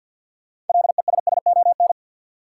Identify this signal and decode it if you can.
Morse code, heard as a keyed tone, about 53 wpm, 700 Hz. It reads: LESSON
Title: Sentences from Top 1200 Words - New Word Drill - 50wpm